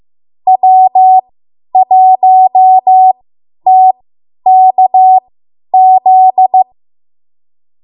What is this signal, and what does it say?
Morse code, heard as a keyed tone, about 15 wpm, spaced as for 10 wpm, 750 Hz. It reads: W1TKZ
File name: Morse / 10WPM